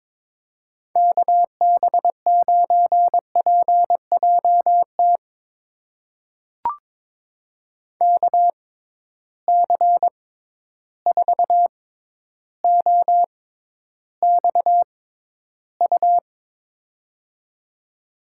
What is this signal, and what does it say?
Morse code, heard as a keyed tone, about 22 wpm, spaced as for 11 wpm, 700 Hz. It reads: KB9PJT KC4OXU
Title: USA Call Signs - 6 Characters - 6x Character Spacing - 22wpm